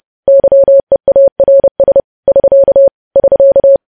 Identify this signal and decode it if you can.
Morse code, heard as a keyed tone, about 30 wpm, 550 Hz. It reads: YEARS <SK> <SK>
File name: podcast / News headlines-1x-30x30